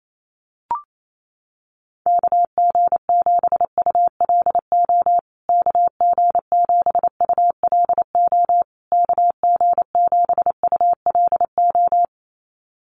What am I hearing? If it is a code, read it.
Morse code, heard as a keyed tone, about 28 wpm, 700 Hz. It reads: KG7ULO KG7ULO KG7ULO
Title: USA Call Signs Repeated 3x 28wpm